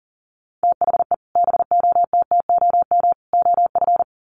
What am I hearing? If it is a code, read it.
Morse code, heard as a keyed tone, about 40 wpm, 700 Hz. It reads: THE BOTTOM OF
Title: Sets of 3 Words 40wpm